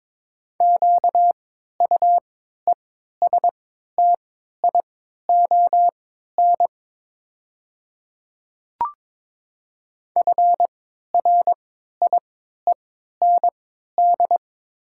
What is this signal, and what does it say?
Morse code, heard as a keyed tone, about 22 wpm, spaced as for 12 wpm, 700 Hz. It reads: QUESTION FRIEND
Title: 3x Character Spacing - Top 500 Words - 22wpm